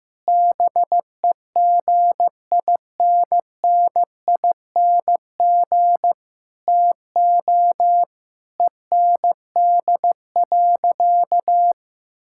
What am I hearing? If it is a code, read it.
Morse code, heard as a keyed tone, about 15 wpm, 700 Hz. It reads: BEGINNING TO END.
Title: Sentences from Top 1000 Words - Review Part 5 - 15wpm